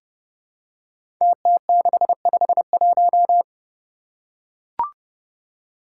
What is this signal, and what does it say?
Morse code, heard as a keyed tone, about 30 wpm, 700 Hz. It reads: TT651
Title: Five Challenging Letter-Number Combinations 30wpm